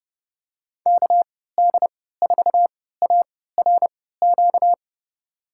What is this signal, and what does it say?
Morse code, heard as a keyed tone, about 31 wpm, spaced as for 17 wpm, 700 Hz. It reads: KD4ARQ
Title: USA Call Signs - 6 Characters - 3x Character Spacing - 30wpm